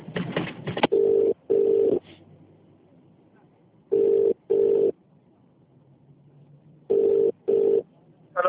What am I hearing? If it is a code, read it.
Ringback tone (UK), 400 + 450 Hz.